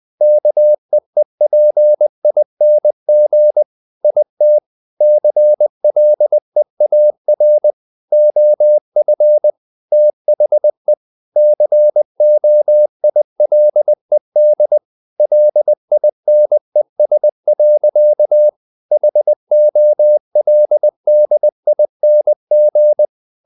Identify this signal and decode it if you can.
Morse code, heard as a keyed tone, about 20 wpm, 600 Hz. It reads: KEEPING IT CLEAR OF THE COILED LINES. HOLDING
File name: the old man and the sea / the old man and the sea part 1 20wpm